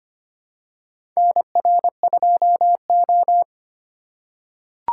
Morse code, heard as a keyed tone, about 25 wpm, 700 Hz. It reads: NR2O
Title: USA Call Signs - 4 Characters - 4x Character Spacing - 25wpm